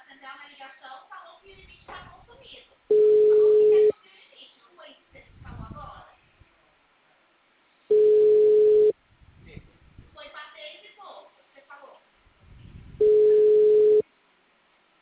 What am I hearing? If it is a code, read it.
Ringback tone (Europe, 425 Hz), with speech.